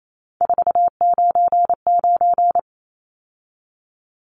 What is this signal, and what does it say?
Morse code, heard as a keyed tone, about 28 wpm, 700 Hz. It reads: 499
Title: Numbers - 3-Digit 28wpm